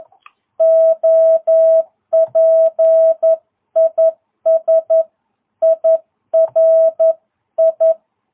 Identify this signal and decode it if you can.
Morse code, heard as a keyed tone, about 11 wpm, 660 Hz. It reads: OP IS IRI
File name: jh8cbh / 2022cw